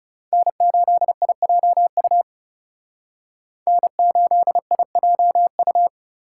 Morse code, heard as a keyed tone, about 38 wpm, 700 Hz. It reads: N8IJU N8IJU